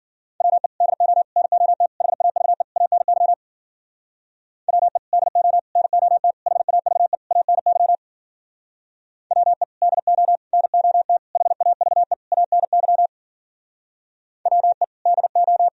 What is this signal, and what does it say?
Morse code, heard as a keyed tone, about 50 wpm, 700 Hz. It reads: WE DO NOT HAVE ANY WE DO NOT HAVE ANY WE DO NOT HAVE ANY WE DO
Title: Speed-Racing - Sets of 5 Words 40wpm